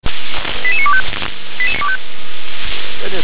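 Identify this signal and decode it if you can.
Five-tone selective call (ZVEI), heard as quick runs of five tones, tones of about 70 ms. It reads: 88925
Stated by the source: ZVEI five-tone selective call received by SDR.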